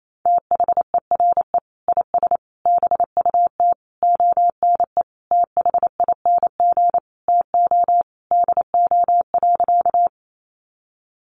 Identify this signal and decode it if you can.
Morse code, heard as a keyed tone, about 28 wpm, 700 Hz. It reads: THERE IS BUT ONE THING TO DO.
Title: Sentences from Top 100 Words 28wpm